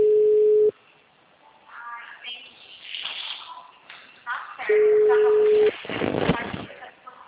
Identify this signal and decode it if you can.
Ringback tone (Europe, 425 Hz), with speech.
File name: out- -2032- - - .62119